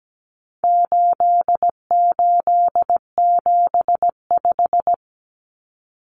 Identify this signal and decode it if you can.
Morse code, heard as a keyed tone, about 17 wpm, 700 Hz. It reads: 8875
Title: Common Cut Numbers - 4-Digit 17wpm